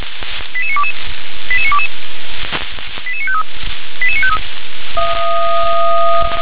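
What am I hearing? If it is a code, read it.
Five-tone selective call (ZVEI), heard as quick runs of five tones, tones of about 70 ms. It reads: 88922 88922 88953 88953
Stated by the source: ZVEI five-tone selective call received on an SDR.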